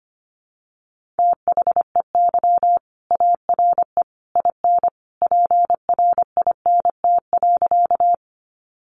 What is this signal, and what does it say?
Morse code, heard as a keyed tone, about 25 wpm, 700 Hz. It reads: THEY ARE IN PRINT.
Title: Sentences from Top 1400 Words 25wpm